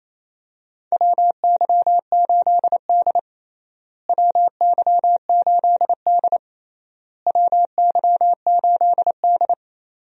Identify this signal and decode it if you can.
Morse code, heard as a keyed tone, about 28 wpm, 700 Hz. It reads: WY8D WY8D WY8D
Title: USA Call Signs - 4 Characters - Repeated 3x - 28wpm